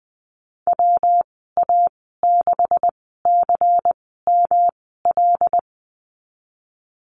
Morse code, heard as a keyed tone, about 20 wpm, spaced as for 14 wpm, 700 Hz. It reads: WA6CML
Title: 193 Lesson 31g - Letter v - Call Signs 20wpm